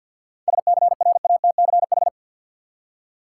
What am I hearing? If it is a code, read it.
Morse code, heard as a keyed tone, about 53 wpm, 700 Hz. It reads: SCRATCH
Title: Seven Letter Words - 50wpm